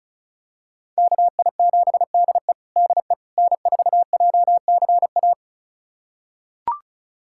Morse code, heard as a keyed tone, about 36 wpm, 700 Hz. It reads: KI7DE DE N4JCA
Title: USA Callsign DE Callsign - 5 Characters - 35wpm